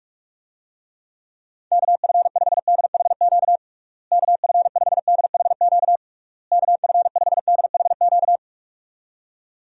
Morse code, heard as a keyed tone, about 46 wpm, 700 Hz. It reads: KF5DHQ KF5DHQ KF5DHQ
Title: USA Call Signs - 6 Characters - Repeated 3x - 45wpm